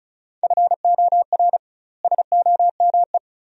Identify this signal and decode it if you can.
Morse code, heard as a keyed tone, about 36 wpm, 700 Hz. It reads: FOR SOME
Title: 2x Word-Spacing - Sentences from Top 200 Words - 35wpm